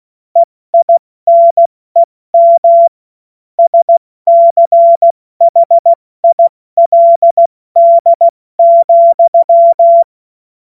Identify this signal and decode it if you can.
Morse code, heard as a keyed tone, about 16 wpm, 680 Hz. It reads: EINEM SCHILD,
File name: Morsekurs / Witze / Witz 07. Juni 2022 70 F80